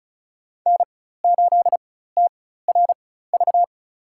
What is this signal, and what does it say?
Morse code, heard as a keyed tone, about 34 wpm, spaced as for 16 wpm, 700 Hz. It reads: N8TRV
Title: USA Call Signs - 5 Characters - 4x Character Spacing - 35wpm